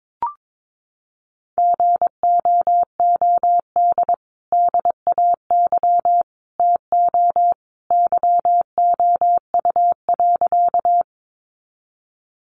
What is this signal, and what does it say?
Morse code, heard as a keyed tone, about 22 wpm, 700 Hz. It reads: GOOD DAY TO YOU.
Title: Sentences from Top 200 Words 22wpm